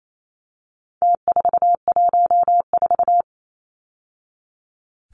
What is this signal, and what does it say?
Morse code, heard as a keyed tone, about 28 wpm, 700 Hz. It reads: T414
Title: Numbers - Contest Style - 4-Digit 28wpm